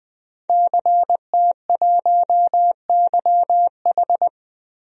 Morse code, heard as a keyed tone, about 20 wpm, 700 Hz. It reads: CT1YH
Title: DXCC-100 Call Signs - 5 Characters - Repeated 2x - 20wpm